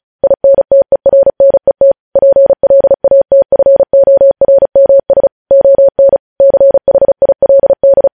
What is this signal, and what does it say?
Morse code, heard as a keyed tone, about 36 wpm, 550 Hz. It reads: INTERNET PLATFORMS ON CHILD